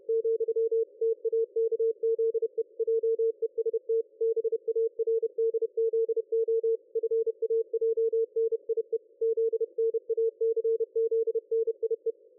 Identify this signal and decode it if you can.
Morse code, heard as a keyed tone, about 31 wpm, 450 Hz. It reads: , TAKZE JEST BARDZO FAJNIE ZNACZNIE
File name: dlazaawansowanych / nagraniazpasma / ExpertSDR2 line out 23.01.2020 14.38.27